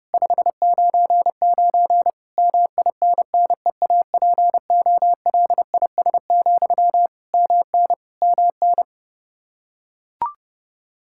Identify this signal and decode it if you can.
Morse code, heard as a keyed tone, about 32 wpm, 700 Hz. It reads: 599 MINNEAPOLIS, MN MN
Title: Practice QSO - Exchange 1 - Beginning - 30wpm